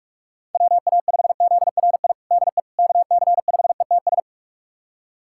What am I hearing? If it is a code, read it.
Morse code, heard as a keyed tone, about 46 wpm, 700 Hz. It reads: WA5ZRI DE KC5ETS
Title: USA Callsign DE Callsign 45wpm